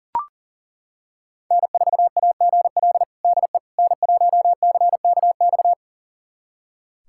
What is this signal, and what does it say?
Morse code, heard as a keyed tone, about 40 wpm, 700 Hz. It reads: N4AGL DE N1CKX
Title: USA Callsign DE Callsign - 5 Characters - 40wpm